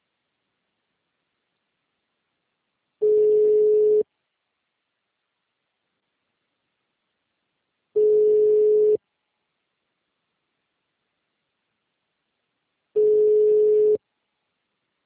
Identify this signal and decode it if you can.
Ringback tone (Europe, 425 Hz).
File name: out- -2010- - - .51753